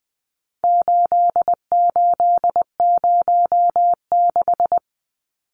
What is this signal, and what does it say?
Morse code, heard as a keyed tone, about 20 wpm, 700 Hz. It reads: 8806
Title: Numbers - 4-Digit 20wpm